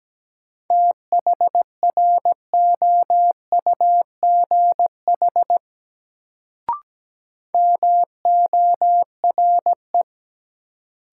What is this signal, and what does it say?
Morse code, heard as a keyed tone, about 17 wpm, 700 Hz. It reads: THROUGH MORE
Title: Sets of 2 Words - New Word Drill - 17wpm